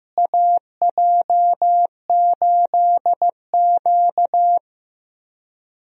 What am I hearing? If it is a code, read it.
Morse code, heard as a keyed tone, about 15 wpm, 700 Hz. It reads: AJ8Q
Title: USA Call Signs - 4 Characters - Repeated 3x - 15wpm